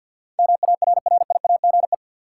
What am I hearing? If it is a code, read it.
Morse code, heard as a keyed tone, about 47 wpm, 700 Hz. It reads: MARRIAGE
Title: Sentences from Top 800 Words - New Word Drill - 50wpm